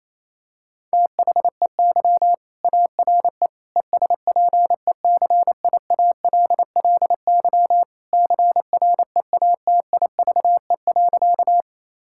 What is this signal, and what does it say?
Morse code, heard as a keyed tone, about 27 wpm, 700 Hz. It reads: THEY ARE ESPECIALLY CREATIVE.
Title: Sentences from Top 1000 Words - Review Part 5 - 28wpm